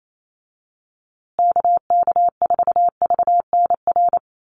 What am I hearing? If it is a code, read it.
Morse code, heard as a keyed tone, about 28 wpm, 700 Hz. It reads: KK4VNR
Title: USA Call Signs Repeated 2x 28wpm